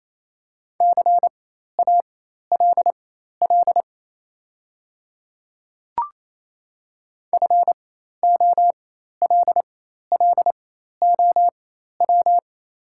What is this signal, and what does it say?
Morse code, heard as a keyed tone, about 28 wpm, spaced as for 13 wpm, 700 Hz. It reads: CALL FOLLOW
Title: 4x Character Spacing - Top 300 Words - 28wpm